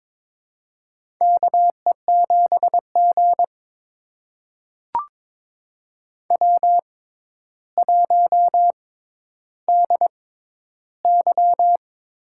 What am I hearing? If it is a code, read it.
Morse code, heard as a keyed tone, about 22 wpm, spaced as for 11 wpm, 700 Hz. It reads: KE7G W1DY
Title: USA Call Signs - 4 Characters - 6x Character Spacing - 22wpm